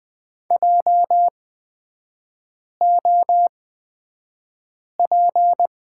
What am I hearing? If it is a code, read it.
Morse code, heard as a keyed tone, about 20 wpm, spaced as for 5 wpm, 690 Hz. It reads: JOP
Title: 229 Lesson 37e - Letter x - Three Characters 20wpm